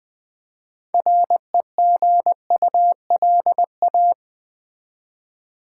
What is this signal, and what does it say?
Morse code, heard as a keyed tone, about 20 wpm, 700 Hz. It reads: REGULA
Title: Build A Word - 3 Letters At A Time - 9-Letter Words - 20wpm